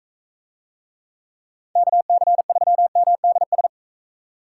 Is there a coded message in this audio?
Yes: Morse code from 1.8 to 3.7 s, heard as a keyed tone, about 41 wpm, 700 Hz. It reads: KC3MDS